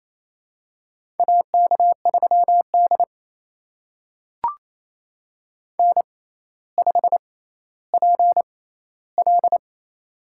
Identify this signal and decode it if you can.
Morse code, heard as a keyed tone, about 28 wpm, spaced as for 14 wpm, 700 Hz. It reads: AK3D N5PL
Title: USA Call Signs - 4 Characters - 6x Character Spacing - 28wpm